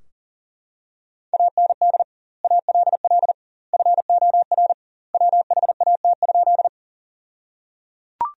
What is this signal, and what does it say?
Morse code, heard as a keyed tone, about 37 wpm, 700 Hz. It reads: AND ALL FOR WHAT?
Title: 2x Word-Spacing - Sentences from Top 100 Words - 40wpm